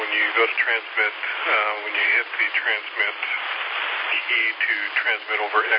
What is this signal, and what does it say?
Emergency Alert System (SAME) tones (header bursts), heard as harsh buzzing bursts, with speech. It is heard from 0.6 to 5.4 s.